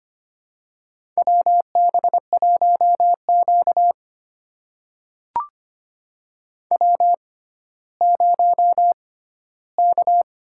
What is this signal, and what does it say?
Morse code, heard as a keyed tone, about 25 wpm, spaced as for 14 wpm, 700 Hz. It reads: WB1Q W0K
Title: USA Call Signs - 6x Character Spacing - 25wpm